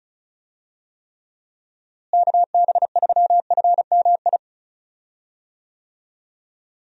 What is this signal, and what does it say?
Morse code, heard as a keyed tone, about 36 wpm, 700 Hz. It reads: KB3FMI
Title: USA Call Signs - 6 Characters - Repeated 2x - 35wpm